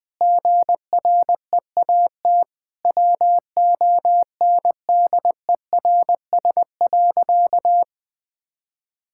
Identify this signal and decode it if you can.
Morse code, heard as a keyed tone, about 20 wpm, 700 Hz. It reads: GREAT WONDERS.
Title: Sentences with 4-Word Phrases - Spoken Beginning - 20wpm